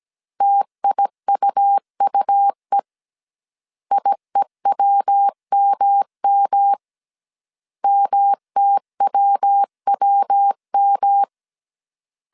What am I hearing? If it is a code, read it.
Morse code, heard as a keyed tone, about 17 wpm, 800 Hz. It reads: TIUUE IEWMM MTWWM